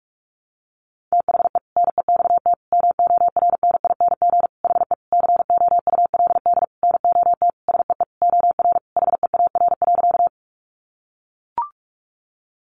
Morse code, heard as a keyed tone, about 44 wpm, 700 Hz. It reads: THE NEXT MORNING HE COULD NOT SEE OR HEAR.
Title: Sentences from Top 600 Words 45wpm